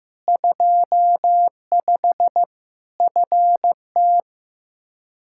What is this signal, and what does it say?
Morse code, heard as a keyed tone, about 16 wpm, 700 Hz. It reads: 25 FT
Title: Practice QSO - Exchange 2 - Info - 15wpm